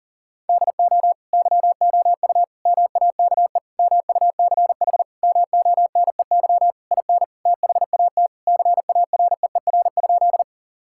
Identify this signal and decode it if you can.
Morse code, heard as a keyed tone, about 40 wpm, 700 Hz. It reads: DO YOU MAKE MUCH MONEY IN THAT CAREER?
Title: Sentences from Top 1000 Words - Review Part 6 - 40wpm